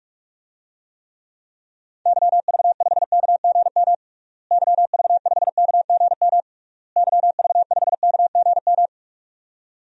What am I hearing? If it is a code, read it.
Morse code, heard as a keyed tone, about 46 wpm, 700 Hz. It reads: YV5KGM YV5KGM YV5KGM
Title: DXCC-100 Call Signs - 6 Characters - Repeated 3x - 45wpm